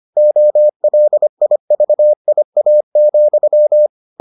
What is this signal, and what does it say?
Morse code, heard as a keyed tone, about 25 wpm, 600 Hz. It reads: OLIVIA,